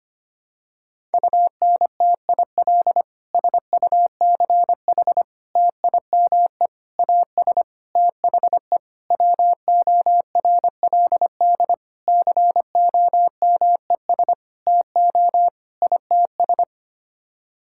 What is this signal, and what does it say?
Morse code, heard as a keyed tone, about 25 wpm, 700 Hz. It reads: UNTIL SUCH TIME AS THE WORLD COMES TO ITS
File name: Sets of 9 Words - 25wpm